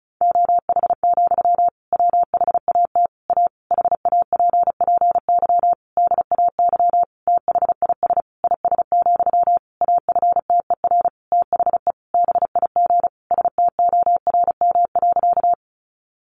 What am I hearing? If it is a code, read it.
Morse code, heard as a keyed tone, about 34 wpm, 700 Hz. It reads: OH, WHAT A HAPPY DAY THIS IS, AFTER THE BIG STORM.
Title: Sentences from Top 1500 Words 35wpm